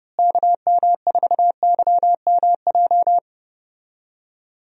Morse code, heard as a keyed tone, about 31 wpm, 700 Hz. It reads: KM4YMJ
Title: USA Call Signs - 6 Characters - Repeated 2x - 30wpm